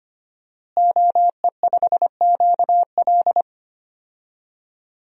Morse code, heard as a keyed tone, about 25 wpm, 700 Hz. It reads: OE5QL
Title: DXCC-100 Call Signs - 5 Characters - Repeated 2x - 25wpm